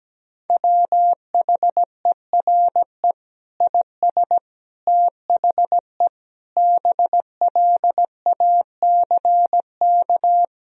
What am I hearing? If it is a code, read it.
Morse code, heard as a keyed tone, about 17 wpm, 700 Hz. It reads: WHERE IS THE BLACK
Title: Sentences from Top 1000 Words - Review Part 2 - 17wpm